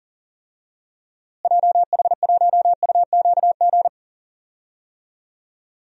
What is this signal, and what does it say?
Morse code, heard as a keyed tone, about 40 wpm, 700 Hz. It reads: JH1UQG
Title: DXCC-100 Call Signs - 6 Characters - Repeated 3x - 40wpm